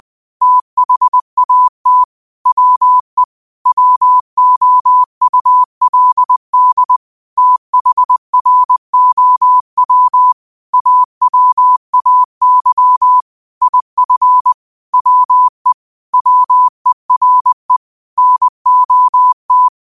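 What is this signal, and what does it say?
Morse code, heard as a keyed tone, about 20 wpm, 1000 Hz. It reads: THAT WE WOULD THROW AWAY IF WE WERE NOT
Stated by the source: Quote of the day sent in Morse at 20 words per minute.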